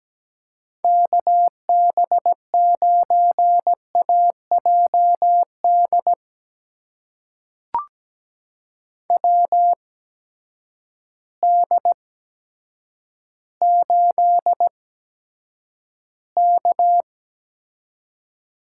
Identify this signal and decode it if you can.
Morse code, heard as a keyed tone, about 17 wpm, spaced as for 8 wpm, 700 Hz. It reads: KB9AJD WD8K
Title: USA Call Signs - 6 Characters - 8x Character Spacing - 17wpm